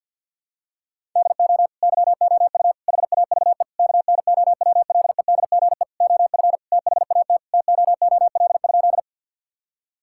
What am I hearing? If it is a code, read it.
Morse code, heard as a keyed tone, about 51 wpm, 700 Hz. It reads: DO YOU HAVE KNOWLEDGE OF THAT TOOL?
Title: Sentences from Top 1000 Words - Review Part 6 - 50wpm